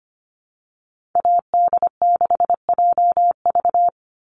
Morse code, heard as a keyed tone, about 25 wpm, 700 Hz. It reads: AD6JV
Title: Call Signs Repeated 3x - 25wpm